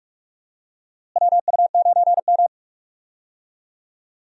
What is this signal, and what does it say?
Morse code, heard as a keyed tone, about 45 wpm, 700 Hz. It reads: WU9M
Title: USA Call Signs - 4 Characters - Repeated 2x - 45wpm